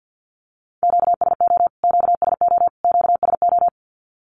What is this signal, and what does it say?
Morse code, heard as a keyed tone, about 50 wpm, 700 Hz. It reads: QSO QSO QSO
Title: Q-Codes Repeated 3x 50wpm